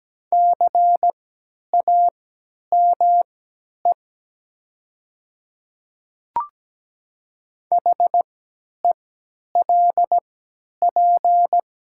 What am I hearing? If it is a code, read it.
Morse code, heard as a keyed tone, about 17 wpm, spaced as for 10 wpm, 700 Hz. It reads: CAME HELP
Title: 3x Character Spacing - Top 300 Words - 17wpm